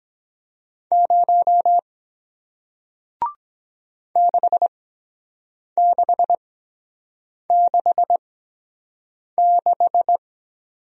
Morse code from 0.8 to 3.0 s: 0

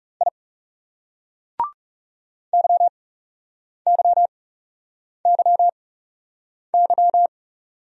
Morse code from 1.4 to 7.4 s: YYYY

